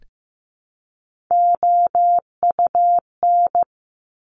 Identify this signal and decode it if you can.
Morse code, heard as a keyed tone, about 15 wpm, 700 Hz. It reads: OUN
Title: Top Three Letter Combinations in Words 15wpm